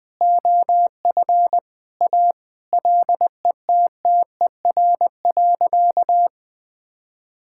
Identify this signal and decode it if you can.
Morse code, heard as a keyed tone, about 20 wpm, 700 Hz. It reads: OF A LETTER.
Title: Sentences with 5-Word Phrases - Spoken Beginning - 20wpm